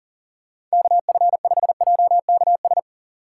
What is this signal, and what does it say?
Morse code, heard as a keyed tone, about 40 wpm, 700 Hz. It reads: KF5JKS
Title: USA Call Signs - 6 Characters - 3x Character Spacing - 40wpm